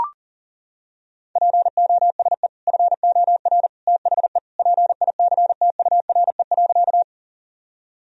Morse code, heard as a keyed tone, about 40 wpm, 700 Hz. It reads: POSE FOR THE PICTURE.
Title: Sentences from Top 1000 Words - Review Part 2 - 40wpm